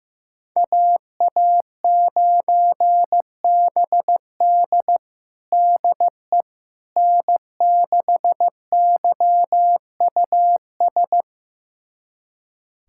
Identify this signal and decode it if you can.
Morse code, heard as a keyed tone, about 15 wpm, 700 Hz. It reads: AA9BD DE N6YUS
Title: USA Callsign DE Callsign 15wpm